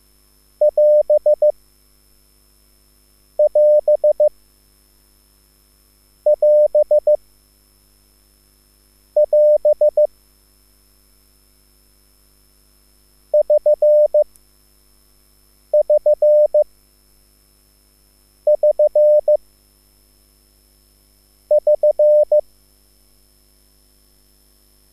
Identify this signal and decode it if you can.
Morse code, heard as a keyed tone, about 15 wpm, spaced as for 4 wpm, 600 Hz. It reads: <AS><AS><AS><AS> <SN><SN><SN><SN>